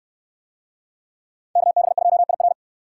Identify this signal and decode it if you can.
Morse code, heard as a keyed tone, about 66 wpm, 700 Hz. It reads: KB2IR